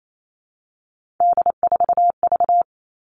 Morse code, heard as a keyed tone, about 28 wpm, 700 Hz. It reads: D4V
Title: All Cut Numbers - 3-Digit 28wpm